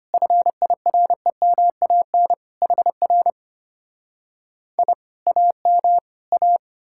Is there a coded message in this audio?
Yes: Morse code from 0.1 to 6.6 s, heard as a keyed tone, about 33 wpm, 700 Hz. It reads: FIREMAN HR I AM A